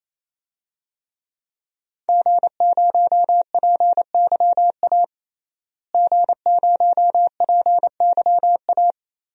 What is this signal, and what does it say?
Morse code, heard as a keyed tone, about 27 wpm, 700 Hz. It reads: G0PYA G0PYA
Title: DXCC-100 Call Signs - 5 Characters - Repeated 3x - 28wpm